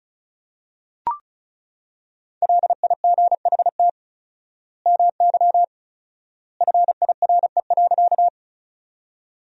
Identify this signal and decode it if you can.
Morse code, heard as a keyed tone, about 36 wpm, 700 Hz. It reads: LIGHT MY FIRE.
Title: 4x Word-Spacing - Sentences from Top 400 Words - 35wpm